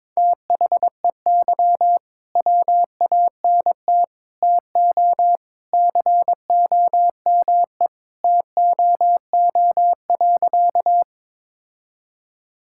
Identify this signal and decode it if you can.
Morse code, heard as a keyed tone, about 21 wpm, 700 Hz. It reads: THEY WANT TO COME TOO.
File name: practice / 22wpm / Sentences with 3-Word Phrases - 22wpm